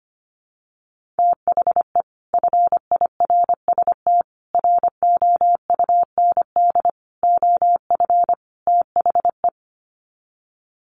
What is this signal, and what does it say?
Morse code, heard as a keyed tone, about 25 wpm, 700 Hz. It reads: THE FIRST ROUND OF THE
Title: Sets of 5 Words Encore 25wpm